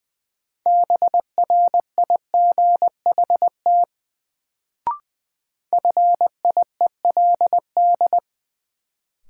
Morse code, heard as a keyed tone, about 20 wpm, 700 Hz. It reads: BRIGHT FIELD
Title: Sentences from Top 1000 Words - New Word Drill - 20wpm